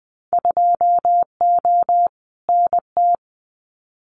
Morse code, heard as a keyed tone, about 20 wpm, 700 Hz. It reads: 2O NT
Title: Winter Field Day Exchanges Encore 20wpm